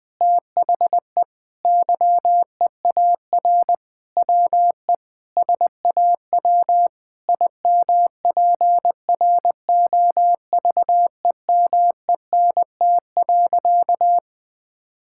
Morse code, heard as a keyed tone, about 20 wpm, 700 Hz. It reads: THE YEAR WE SAW IMPROVEMENT.
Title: Sentences with 8-Word Phrases - 20wpm